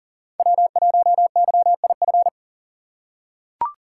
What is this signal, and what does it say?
Morse code, heard as a keyed tone, about 40 wpm, 700 Hz. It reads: W1YIF
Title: USA Call Signs - 5 Characters - 4x Character Spacing - 40wpm